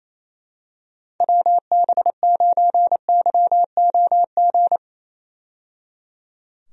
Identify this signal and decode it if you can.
Morse code, heard as a keyed tone, about 27 wpm, 700 Hz. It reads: WB9YOG
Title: USA Call Signs - 6 Characters - 28wpm